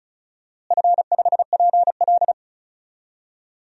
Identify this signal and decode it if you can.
Morse code, heard as a keyed tone, about 36 wpm, 700 Hz. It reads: F5PL
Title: DXCC-100 Call Signs - 4 Characters - Repeated 2x - 35wpm